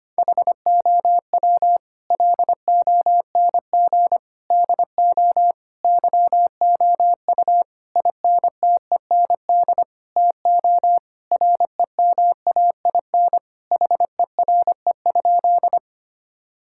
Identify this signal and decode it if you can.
Morse code, heard as a keyed tone, about 25 wpm, 700 Hz. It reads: HOW LONG DO YOU INTEND TO REMAIN HERE?
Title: Sentences from Top 1600 Words 25wpm